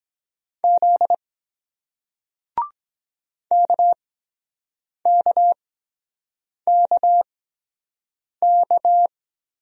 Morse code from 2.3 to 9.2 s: KKKK

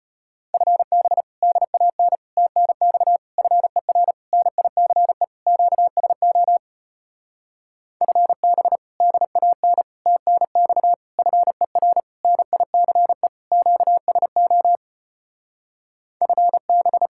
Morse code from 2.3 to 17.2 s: TNX FER NICE QSO FB DAN TNX FER NICE QSO FB

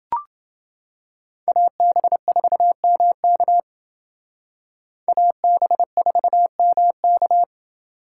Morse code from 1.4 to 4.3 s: AB4MK

